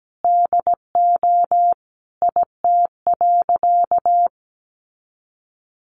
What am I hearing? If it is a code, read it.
Morse code, heard as a keyed tone, about 17 wpm, 700 Hz. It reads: DO IT.
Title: Sentences from Top 100 Words 17wpm